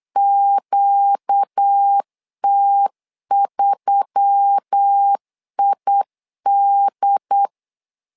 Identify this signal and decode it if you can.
Morse code, heard as a keyed tone, about 8 wpm, 800 Hz. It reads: QT3ID
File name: LEZIONI / 5 LBVC / 5 LBVC 8wpm 2935